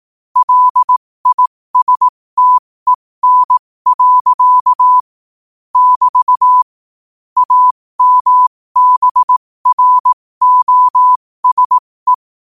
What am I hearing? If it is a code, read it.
Morse code, heard as a keyed tone, about 18 wpm, spaced as for 15 wpm, 1000 Hz. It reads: LISTEN. <BT> AMBROSE
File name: qotdfiles / 15wpm